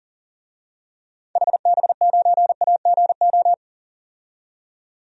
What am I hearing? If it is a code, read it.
Morse code, heard as a keyed tone, about 38 wpm, 700 Hz. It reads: HB9AGO
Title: DXCC-100 Call Signs - 6 Characters - 40wpm